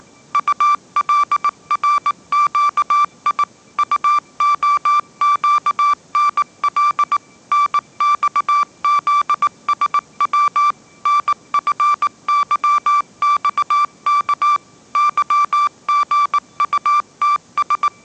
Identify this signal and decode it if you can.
Morse code, heard as a keyed tone, about 18 wpm, 1170 Hz. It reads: ULRQI UOQNL NXZSW NFYXK YGUTS